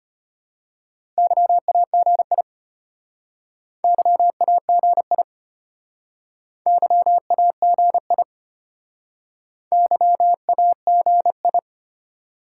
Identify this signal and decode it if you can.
Morse code, heard as a keyed tone, about 31 wpm, 700 Hz. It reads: YAGI YAGI YAGI YAGI
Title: Speed-Racing Top 180 QSO Elements 25wpm